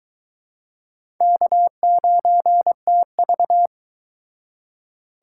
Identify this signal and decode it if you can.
Morse code, heard as a keyed tone, about 23 wpm, 700 Hz. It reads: K9TV